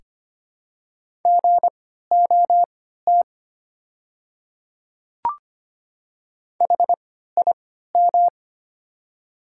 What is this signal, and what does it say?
Morse code, heard as a keyed tone, about 25 wpm, spaced as for 14 wpm, 700 Hz. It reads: GOT HIM
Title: 3x Character Spacing - Top 300 Words - 25wpm